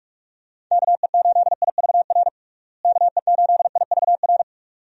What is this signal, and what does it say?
Morse code, heard as a keyed tone, about 46 wpm, 700 Hz. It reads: KE8IVR KE8IVR
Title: USA Call Signs - 6 Characters - Repeated 2x - 45wpm